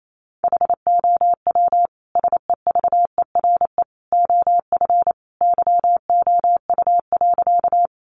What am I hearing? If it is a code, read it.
Morse code, heard as a keyed tone, about 28 wpm, 700 Hz. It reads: HOW SEVERE OF YOU.
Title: Sentences from Top 1400 Words 28wpm